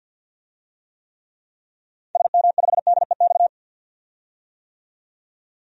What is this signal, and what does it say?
Morse code, heard as a keyed tone, about 51 wpm, 700 Hz. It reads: SM5DEX
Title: DXCC-100 Call Signs - 6 Characters - 50wpm